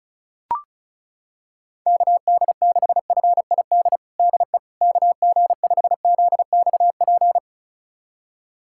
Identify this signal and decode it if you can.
Morse code, heard as a keyed tone, about 36 wpm, 700 Hz. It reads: KD6FID DE KG5ZXP
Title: USA Callsign DE Callsign - 6 Characters - 35wpm